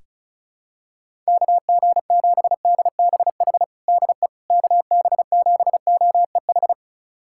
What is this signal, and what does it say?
Morse code, heard as a keyed tone, about 34 wpm, 700 Hz. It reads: KG7DBH DE KB7OEH
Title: USA Callsign DE Callsign - 6 Characters - 35wpm